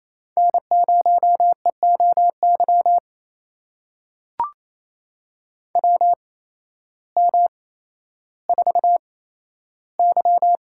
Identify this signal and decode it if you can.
Morse code, heard as a keyed tone, about 28 wpm, spaced as for 13 wpm, 700 Hz. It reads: N0EOY WM4Y
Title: USA Call Signs - 8x Character Spacing - 28wpm